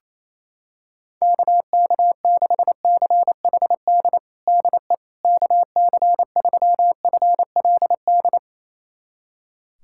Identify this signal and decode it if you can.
Morse code, heard as a keyed tone, about 27 wpm, 700 Hz. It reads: KK6CHD DE KC3FLD
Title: USA Callsign DE Callsign - 6 Characters - 28wpm